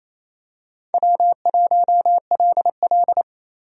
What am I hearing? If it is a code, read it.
Morse code, heard as a keyed tone, about 28 wpm, 700 Hz. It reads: W1LL
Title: USA Call Signs - 4 Characters - 28wpm